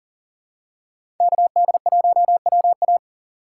Morse code, heard as a keyed tone, about 40 wpm, 700 Hz. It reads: KD1WA
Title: USA Call Signs - 5 Characters - 2x Character Spacing - 40wpm